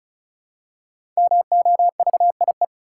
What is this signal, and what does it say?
Morse code, heard as a keyed tone, about 36 wpm, 700 Hz. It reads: MOVIE
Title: Five Letter Words - 35wpm